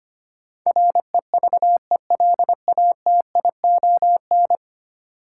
Morse code, heard as a keyed tone, about 25 wpm, 700 Hz. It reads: REVELATION
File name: Ten Letter Words - 25wpm